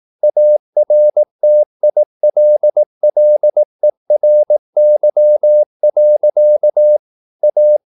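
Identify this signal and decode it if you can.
Morse code, heard as a keyed tone, about 18 wpm, 600 Hz. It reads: ARTILLERY. A
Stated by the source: Morse practice at 18 wpm, English text, from CW med Gnister.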